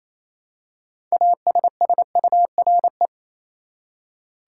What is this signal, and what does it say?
Morse code, heard as a keyed tone, about 28 wpm, 700 Hz. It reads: ASSURE
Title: Six Letter Words - 28wpm